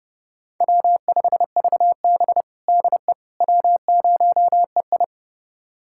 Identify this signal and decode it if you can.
Morse code, heard as a keyed tone, about 32 wpm, 700 Hz. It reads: W5VB DE W0EI
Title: USA Callsign DE Callsign - 4 Characters - 30wpm